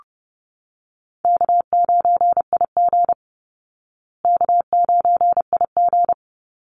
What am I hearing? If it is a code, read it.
Morse code, heard as a keyed tone, about 30 wpm, 700 Hz. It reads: K9IG K9IG
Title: Call Signs Repeated 2x - 30wpm